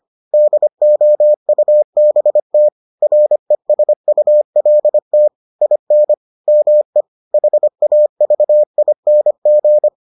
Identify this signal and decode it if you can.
Morse code, heard as a keyed tone, about 25 wpm, 600 Hz. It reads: DOUBT RESULT IN ME HAVING